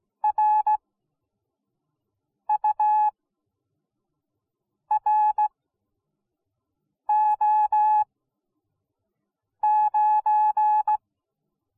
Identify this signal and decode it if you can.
Morse code, heard as a keyed tone, about 17 wpm, spaced as for 5 wpm, 850 Hz. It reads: RURO9